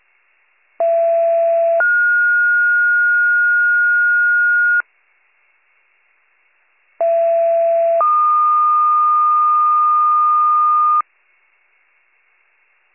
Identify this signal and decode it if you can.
Two-tone page, heard as a one-second tone, then a three-second tone, about 670 then 1430 Hz, about 670 then 1220 Hz. It is heard from 0.8 to 11.0 s.